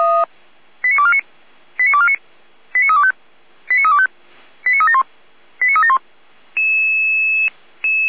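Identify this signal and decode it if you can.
Five-tone selective call (ZVEI), heard as quick runs of five tones, tones of about 70 ms. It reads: 89249 89249 89325 89325 89371 89371
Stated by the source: Five-tone ZVEI selective call received by SDR, from audio.f85360.feuerwehr-orlamuende.org, the Orlamünde fire brigade.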